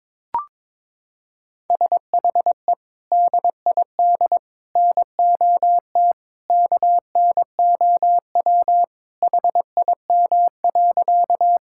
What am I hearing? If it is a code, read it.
Morse code, heard as a keyed tone, about 22 wpm, 700 Hz. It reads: SHE DID NOT KNOW HIM.
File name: practice / 22wpm / Sentences with 3-Word Phrases - 22wpm